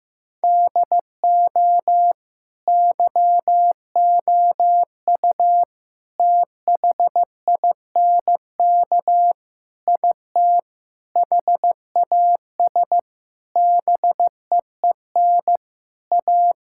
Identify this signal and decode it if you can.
Morse code, heard as a keyed tone, about 15 wpm, 700 Hz. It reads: DO YOU THINK IT HAS BEEN A
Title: Sentences from Top 1000 Words - Review Part 6 - 15wpm